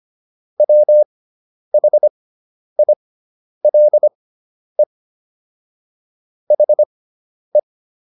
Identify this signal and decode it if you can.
Morse code, heard as a keyed tone, about 25 wpm, spaced as for 10 wpm, 600 Hz. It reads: WHILE HE